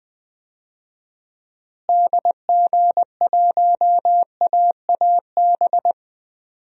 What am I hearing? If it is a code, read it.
Morse code, heard as a keyed tone, about 20 wpm, 700 Hz. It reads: DG1AAB
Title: DXCC-100 Call Signs - 6 Characters - Repeated 2x - 20wpm